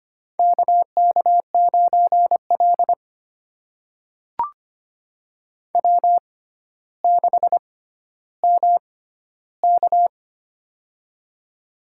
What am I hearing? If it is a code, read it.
Morse code, heard as a keyed tone, about 25 wpm, spaced as for 13 wpm, 700 Hz. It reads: KK9L W6MK
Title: USA Call Signs - 4 Characters - 6x Character Spacing - 25wpm